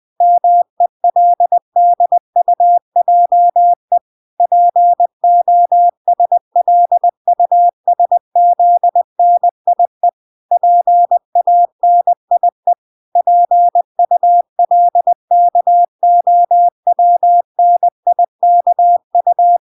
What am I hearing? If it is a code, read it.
Morse code, heard as a keyed tone, about 20 wpm, 700 Hz. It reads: MELDUJE POSLUSZNIE PANIE PULKOWNIKU